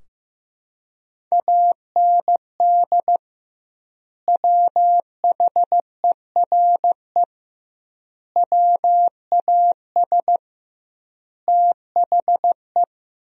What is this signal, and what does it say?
Morse code, heard as a keyed tone, about 16 wpm, 700 Hz. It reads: AND WHERE WAS THE
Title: 2x Word-Spacing - Sentences from Top 200 Words - 15wpm